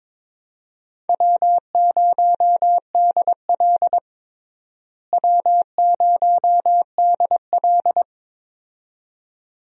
Morse code, heard as a keyed tone, about 21 wpm, 700 Hz. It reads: W0DL W0DL
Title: USA Call Signs - 4 Characters - Repeated 2x - 22wpm